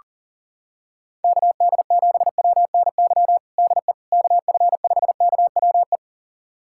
Morse code, heard as a keyed tone, about 40 wpm, 700 Hz. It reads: KD7WNY DE KF5KWE
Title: USA Callsign DE Callsign - 6 Characters - 40wpm